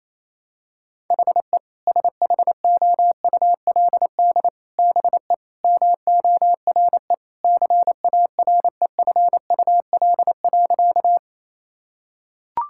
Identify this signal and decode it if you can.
Morse code, heard as a keyed tone, about 28 wpm, 700 Hz. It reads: HE SHOULD BE MORE CAREFUL.
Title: Sentences from Top 1600 Words 28wpm